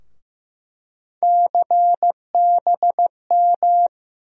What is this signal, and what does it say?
Morse code, heard as a keyed tone, about 15 wpm, 700 Hz. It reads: CBM
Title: USA Call Sign Postfixes - 3 Letters 15wpm